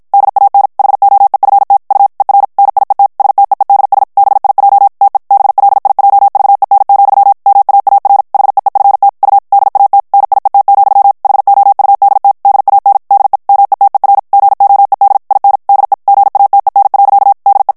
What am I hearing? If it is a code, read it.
Morse code, heard as a keyed tone, about 50 wpm, 800 Hz. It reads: DAN HOEFT U ER NIET STEEDS BIJ TE BLIJVEN, MAAR HEEFT U DAT NIET, HOUDT DAN DE METER GOED IN DE GATEN. DE
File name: HOLLANDAIS texte clair 2 vitesse 50 mots minute